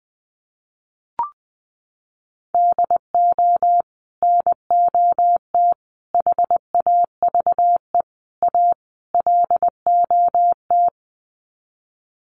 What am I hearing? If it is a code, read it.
Morse code, heard as a keyed tone, about 20 wpm, 700 Hz. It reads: DO NOT HAVE A LOT
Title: Sets of 5 Words Encore 20wpm